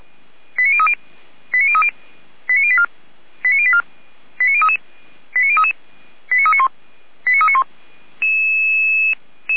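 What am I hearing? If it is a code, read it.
Five-tone selective call (ZVEI), heard as quick runs of five tones, tones of about 80 ms. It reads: 89030 89030 89084 89084 89033 89033 89391 89391
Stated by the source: SDR-received ZVEI five-tone selective call.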